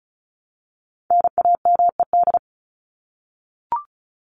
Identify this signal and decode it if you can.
Morse code, heard as a keyed tone, about 34 wpm, 700 Hz. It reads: NAMED
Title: ED Suffix - Top 100 Words with Base Words 35wpm